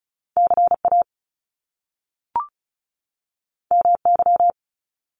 Morse code, heard as a keyed tone, about 36 wpm, 700 Hz. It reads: CA MY
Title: Top Two Letter Combinations in Words 35wpm